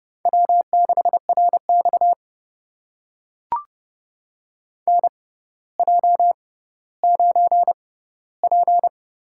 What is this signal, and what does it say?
Morse code, heard as a keyed tone, about 30 wpm, spaced as for 15 wpm, 700 Hz. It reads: W6RX NJ9P
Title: USA Call Signs - 4 Characters - 6x Character Spacing - 30wpm